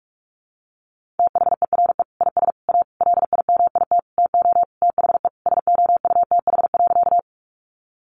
Morse code, heard as a keyed tone, about 44 wpm, 700 Hz. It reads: THERE IS A LIMIT TO THE SOUTH.
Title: Sentences from Top 1500 Words 45wpm